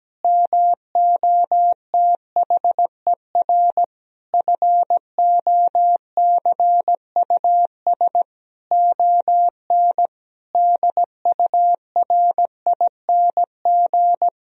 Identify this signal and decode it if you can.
Morse code, heard as a keyed tone, about 17 wpm, 700 Hz. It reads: MOTHER FOCUS ON DURING
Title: Sentences from Top 1000 Words - Review Part 4 - 17wpm